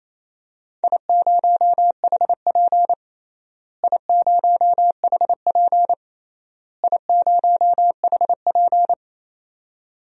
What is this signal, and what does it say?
Morse code, heard as a keyed tone, about 28 wpm, 700 Hz. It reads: I0HP I0HP I0HP
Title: DXCC-100 Call Signs - 4 Characters - Repeated 3x - 28wpm